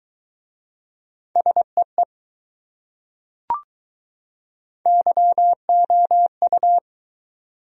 Morse code, heard as a keyed tone, about 20 wpm, 700 Hz. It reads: SEE YOU